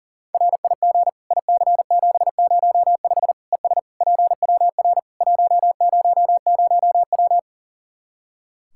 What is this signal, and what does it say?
Morse code, heard as a keyed tone, about 40 wpm, 700 Hz. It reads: RIG IC705 ES PWR 100W
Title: Practice QSO - Exchange 2 - Info - 40wpm